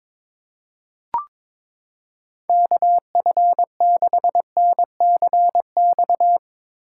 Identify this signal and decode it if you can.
Morse code, heard as a keyed tone, about 22 wpm, 700 Hz. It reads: KF6NCX
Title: K1USN SST QSO Practice - 22wpm